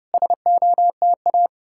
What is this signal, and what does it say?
Morse code, heard as a keyed tone, about 35 wpm, 700 Hz. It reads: SOTA